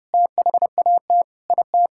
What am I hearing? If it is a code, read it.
Morse code, heard as a keyed tone, about 30 wpm, 700 Hz. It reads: THAT IT
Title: Sentences from Top 1600 Words 30wpm